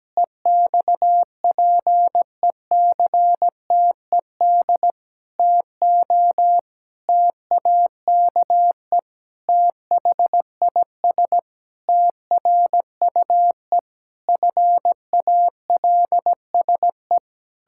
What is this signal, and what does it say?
Morse code, heard as a keyed tone, about 17 wpm, 700 Hz. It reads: EXPECTED TO TAKE THIS TRUE FALSE
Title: Pangram Challenge v2 - 17wpm